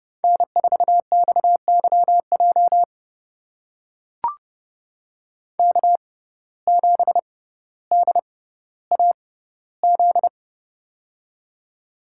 Morse code, heard as a keyed tone, about 30 wpm, spaced as for 15 wpm, 700 Hz. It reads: N4XYJ K7DAZ
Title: USA Call Signs - 5 Characters - 6x Character Spacing - 30wpm